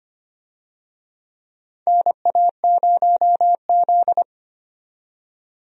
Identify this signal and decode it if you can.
Morse code, heard as a keyed tone, about 25 wpm, 700 Hz. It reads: NA0Z
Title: USA Call Signs - 4 Characters - 25wpm